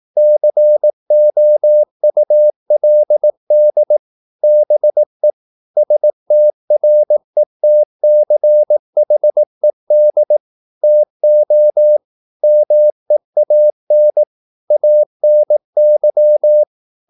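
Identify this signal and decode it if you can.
Morse code, heard as a keyed tone, about 18 wpm, 600 Hz. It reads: COULD BE STRETCHED TO MEAN ANY
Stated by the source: Morse practice at 18 wpm, English text, from CW med Gnister.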